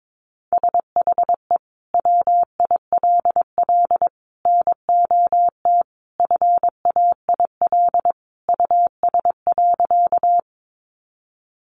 Morse code, heard as a keyed tone, about 22 wpm, 700 Hz. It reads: SHE WILL NOT FAIL US.
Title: Sentences from Top 1100 Words 22wpm